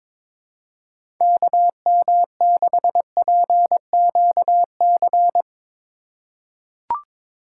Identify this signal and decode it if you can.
Morse code, heard as a keyed tone, about 22 wpm, 700 Hz. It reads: KM6PQC
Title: USA Call Signs - 6 Characters - 4x Character Spacing - 22wpm